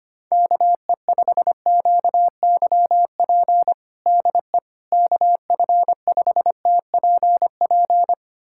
Morse code, heard as a keyed tone, about 25 wpm, 700 Hz. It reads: KE5QYP DE KF5TPP
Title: USA Callsign DE Callsign - 6 Characters - 25wpm